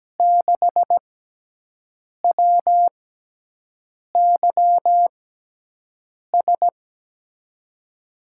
Morse code, heard as a keyed tone, about 17 wpm, spaced as for 6 wpm, 700 Hz. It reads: 6WYS
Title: USA Call Signs - 5 Characters - 6x Character Spacing - 17wpm